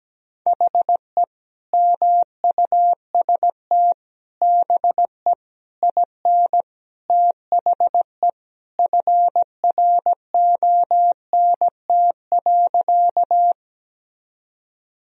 Morse code, heard as a keyed tone, about 17 wpm, 700 Hz. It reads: HE MUST BE IN THE FRONT.
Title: Sentences from Top 1000 Words - Review Part 2 - 17wpm